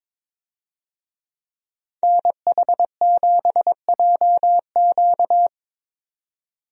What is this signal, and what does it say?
Morse code, heard as a keyed tone, about 22 wpm, 700 Hz. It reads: NH7JQ
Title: USA Call Signs - 5 Characters - 22wpm